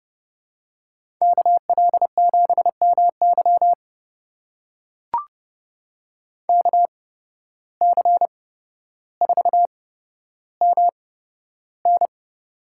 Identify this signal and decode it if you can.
Morse code, heard as a keyed tone, about 31 wpm, spaced as for 13 wpm, 700 Hz. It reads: KL7MY KC4MN
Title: USA Call Signs - 5 Characters - 8x Character Spacing - 30wpm